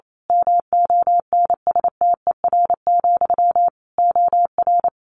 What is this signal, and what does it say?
Morse code, heard as a keyed tone, about 28 wpm, 700 Hz. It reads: MONSTER, OR
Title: A Sound Of Thunder 28wpm